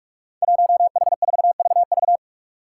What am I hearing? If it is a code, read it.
Morse code, heard as a keyed tone, about 46 wpm, 700 Hz. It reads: 1H4VV
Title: Five Challenging Letter-Number Combinations 45wpm